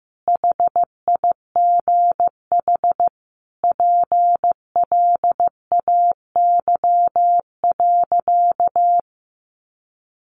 Morse code, heard as a keyed tone, about 15 wpm, 700 Hz. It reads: HIGH PLAY.
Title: Sentences from Top 1500 Words 15wpm